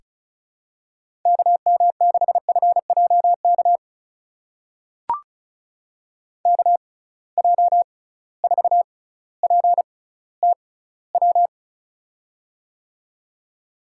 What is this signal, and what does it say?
Morse code, heard as a keyed tone, about 36 wpm, spaced as for 18 wpm, 700 Hz. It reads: KM6FJK KJ4PTW
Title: USA Call Signs - 6x Character Spacing - 35wpm